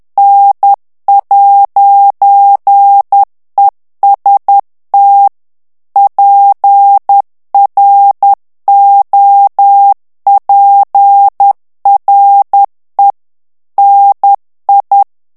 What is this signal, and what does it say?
Morse code, heard as a keyed tone, about 10 wpm, 800 Hz. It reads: N'EST PROPRE NI
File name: morse mp3 / FRANCAIS TEXTE CLAIR 2 VITESSE 10 MOTS MINUTES